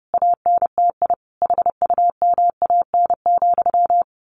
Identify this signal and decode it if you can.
Morse code, heard as a keyed tone, about 30 wpm, 700 Hz. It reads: ANTI HUMAN,
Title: A Sound Of Thunder 30wpm